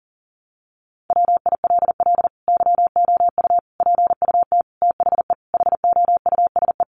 Morse code, heard as a keyed tone, about 37 wpm, 700 Hz. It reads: WILL YOU PUT THE HOUSE
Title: Sentences from Top 1400 Words 40wpm